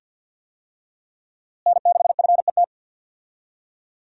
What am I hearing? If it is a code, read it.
Morse code, heard as a keyed tone, about 52 wpm, 700 Hz. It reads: N6FET
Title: USA Call Signs - 5 Characters - Repeated 2x - 50wpm